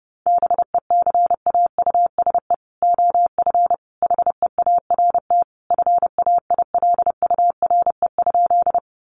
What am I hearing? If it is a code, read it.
Morse code, heard as a keyed tone, about 30 wpm, 700 Hz. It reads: BECAUSE OF HEART FAILURE?
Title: Sentences from Top 1200 Words 30wpm